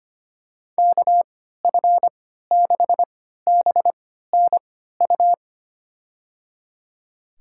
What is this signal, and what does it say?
Morse code, heard as a keyed tone, about 25 wpm, spaced as for 14 wpm, 700 Hz. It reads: KF6BNU
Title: USA Call Signs - 3x Character Spacing - 25wpm